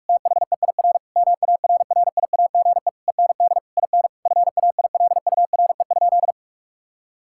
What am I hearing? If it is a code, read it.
Morse code, heard as a keyed tone, about 46 wpm, 700 Hz. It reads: THEIR MARRIAGE END IN FAILURE?
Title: Sentences from Top 1000 Words - Review Part 6 - 45wpm